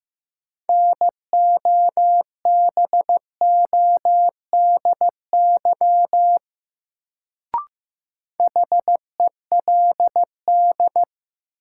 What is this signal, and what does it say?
Morse code, heard as a keyed tone, about 15 wpm, 700 Hz. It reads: NOBODY HELD
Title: Sentences from Top 1600 Words - New Word Drill Alt - 15wpm